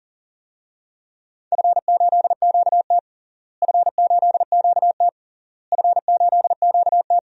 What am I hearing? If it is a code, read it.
Morse code, heard as a keyed tone, about 40 wpm, 700 Hz. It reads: F8QT F8QT F8QT
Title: DXCC-100 Call Signs - 4 Characters - Repeated 3x - 40wpm